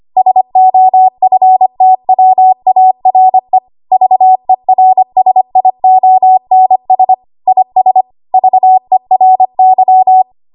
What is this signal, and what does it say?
Morse code, heard as a keyed tone, about 25 wpm, 750 Hz. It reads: SOFTWARE VERSIONS IS VERY
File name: Archive / 25 WPM / 25WPM